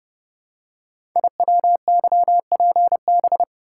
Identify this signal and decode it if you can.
Morse code, heard as a keyed tone, about 30 wpm, 700 Hz. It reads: IWYPB
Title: Code Groups - 5 Letters - 30wpm